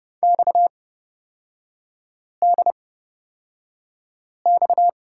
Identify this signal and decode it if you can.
Morse code, heard as a keyed tone, about 31 wpm, spaced as for 5 wpm, 700 Hz. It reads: XDX
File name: Lesson 37e - Letter x - Three Characters 30wpm